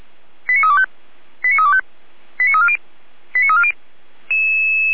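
Five-tone selective call (ZVEI), heard as quick runs of five tones, tones of about 70 ms. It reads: 89325 89325 89340 89340
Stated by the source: ZVEI five-tone call received on SDR.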